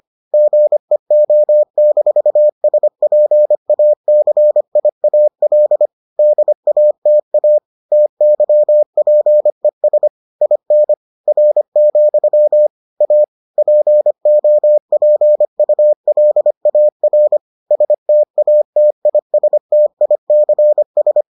Morse code, heard as a keyed tone, about 25 wpm, 600 Hz. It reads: GEO-SPACIAL DATA TYPES IN R, A POPULAR STATISTICS